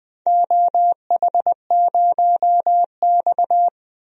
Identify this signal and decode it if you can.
Morse code, heard as a keyed tone, about 20 wpm, 700 Hz. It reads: OH0X
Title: DXCC-100 Call Signs - 4 Characters - 20wpm